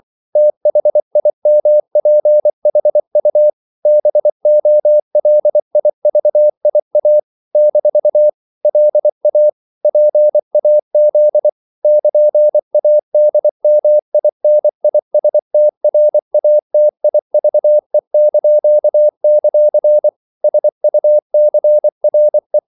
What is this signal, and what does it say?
Morse code, heard as a keyed tone, about 24 wpm, 600 Hz. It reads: THIMPHU BOLIVIA - LA PAZ <KN>ADMINISTRATIVE); SUCRE
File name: CW / Practice / NationalCapitals24